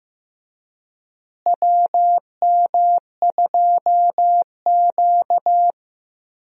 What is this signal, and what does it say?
Morse code, heard as a keyed tone, about 15 wpm, 700 Hz. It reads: WM2Q
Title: USA Call Signs - 4 Characters - Repeated 2x - 15wpm